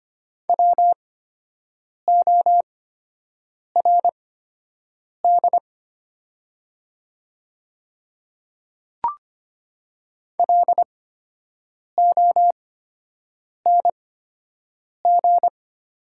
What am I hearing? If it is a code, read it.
Morse code, heard as a keyed tone, about 25 wpm, spaced as for 7 wpm, 700 Hz. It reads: WORD LONG
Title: 8x Character Spacing - Top 200 Words - 25wpm